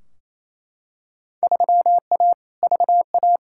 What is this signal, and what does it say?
Morse code, heard as a keyed tone, about 28 wpm, 700 Hz. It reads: 3A VA
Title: Field Day Exchanges Encore - 28wpm